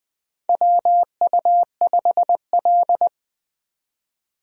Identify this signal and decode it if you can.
Morse code, heard as a keyed tone, about 21 wpm, 700 Hz. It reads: WU5L
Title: USA Call Signs - 4 Characters - 20wpm